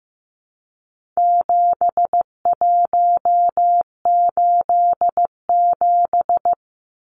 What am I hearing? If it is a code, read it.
Morse code, heard as a keyed tone, about 15 wpm, 700 Hz. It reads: 7187
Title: Numbers - 4-Digit 15wpm